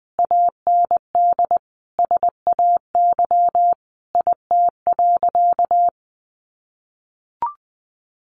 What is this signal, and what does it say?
Morse code, heard as a keyed tone, about 20 wpm, 700 Hz. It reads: AND SAY IT.
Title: Sentences from Top 400 Words 20wpm